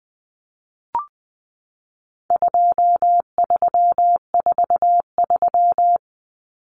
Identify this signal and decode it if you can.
Morse code, heard as a keyed tone, about 20 wpm, 700 Hz. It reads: 2343
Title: Numbers - 4-Digit 20wpm